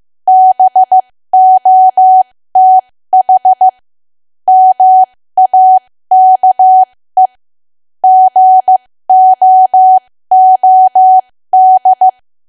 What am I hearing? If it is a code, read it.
Morse code, heard as a keyed tone, about 15 wpm, spaced as for 13 wpm, 750 Hz. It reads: BOTH MAKE GOOD
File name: cwa / W1AW-Level2 / 13WPM